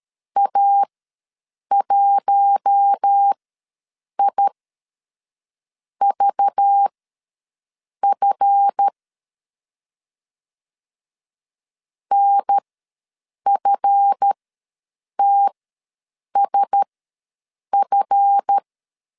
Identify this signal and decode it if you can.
Morse code, heard as a keyed tone, about 13 wpm, spaced as for 6 wpm, 800 Hz. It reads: A1IVF NFTSF
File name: LEZIONI / 7 GF9Z / 7 GF ws 12wpm 3140